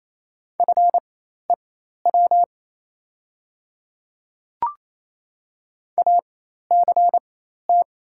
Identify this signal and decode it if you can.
Morse code, heard as a keyed tone, about 28 wpm, spaced as for 13 wpm, 700 Hz. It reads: FEW ACT
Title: 4x Character Spacing - Top 300 Words - 28wpm